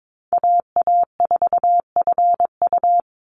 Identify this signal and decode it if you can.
Morse code, heard as a keyed tone, about 22 wpm, 700 Hz. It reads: AA4FU
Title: Call Signs Repeated 2x - 22wpm